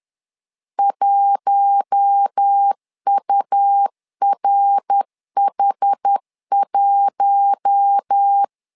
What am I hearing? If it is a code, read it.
Morse code, heard as a keyed tone, about 10 wpm, 800 Hz. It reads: 1URH1
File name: LEZIONI / 4 Q1 10wpm 2448